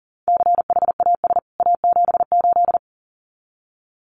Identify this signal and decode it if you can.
Morse code, heard as a keyed tone, about 40 wpm, 700 Hz. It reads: CHAS A78
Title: CWT Exchange 40wpm